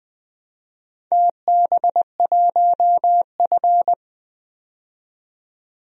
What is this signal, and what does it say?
Morse code, heard as a keyed tone, about 21 wpm, 700 Hz. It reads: TB1F
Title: DXCC-100 Call Signs - 4 Characters - 20wpm